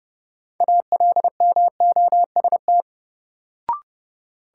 Morse code, heard as a keyed tone, about 29 wpm, 700 Hz. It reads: ALMOST
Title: Sentences from Top 700 Words - New Word Drill - 30wpm